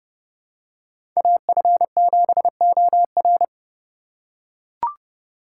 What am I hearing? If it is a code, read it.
Morse code, heard as a keyed tone, about 31 wpm, 700 Hz. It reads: AF7OR
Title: USA Call Signs - 5 Characters - 3x Character Spacing - 30wpm